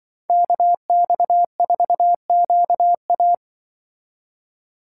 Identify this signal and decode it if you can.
Morse code, heard as a keyed tone, about 24 wpm, 700 Hz. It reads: KX4QA
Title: Speed Racing - USA 2x2 Call Signs 20wpm